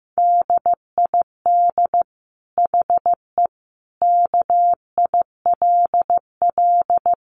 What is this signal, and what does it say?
Morse code, heard as a keyed tone, about 15 wpm, 700 Hz. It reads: DID HE KILL
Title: Sentences from Top 1400 Words 15wpm